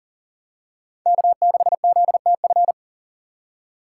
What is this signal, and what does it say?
Morse code, heard as a keyed tone, about 40 wpm, 700 Hz. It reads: K6ZTF
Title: USA Call Signs - 5 Characters - 4x Character Spacing - 40wpm